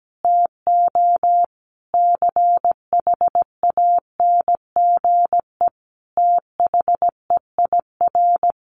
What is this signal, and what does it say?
Morse code, heard as a keyed tone, about 17 wpm, 700 Hz. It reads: TO CHANGE THEIR
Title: Easy Phrases 17wpm